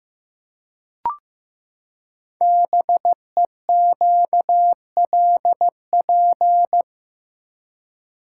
Morse code, heard as a keyed tone, about 15 wpm, 700 Hz. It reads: BEQLP
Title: Code Groups - 5 Letters - 15wpm